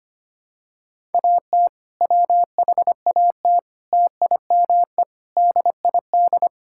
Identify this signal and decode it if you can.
Morse code, heard as a keyed tone, about 25 wpm, 700 Hz. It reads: AT WHAT TIME DID
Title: Sentences from Top 1000 Words - Review Part 6 - 25wpm